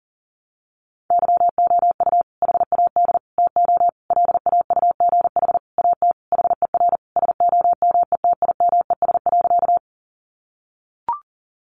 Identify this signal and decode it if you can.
Morse code, heard as a keyed tone, about 40 wpm, 700 Hz. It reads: YOU HAD TO LAUGH AT HER SOMETIMES.
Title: Sentences from Top 600 Words 40wpm